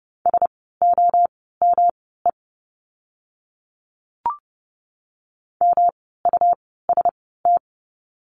Morse code, heard as a keyed tone, about 30 wpm, spaced as for 17 wpm, 700 Hz. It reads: SOME MUST
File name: course / 30:10wpm / 237 Lesson 38f - Letter q - Words 30wpm